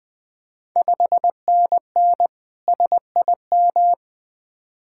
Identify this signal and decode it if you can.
Morse code, heard as a keyed tone, about 20 wpm, 700 Hz. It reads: 5NN SIM
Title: Ontario QSO Party Exchange 20wpm